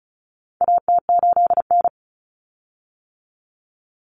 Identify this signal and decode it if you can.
Morse code, heard as a keyed tone, about 34 wpm, 700 Hz. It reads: AT8N
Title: Common Cut Numbers - 4-Digit 35wpm